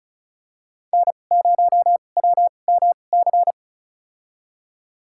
Morse code, heard as a keyed tone, about 36 wpm, spaced as for 25 wpm, 700 Hz. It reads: N0WMC
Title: USA Call Signs - 5 Characters - 2x Character Spacing - 35wpm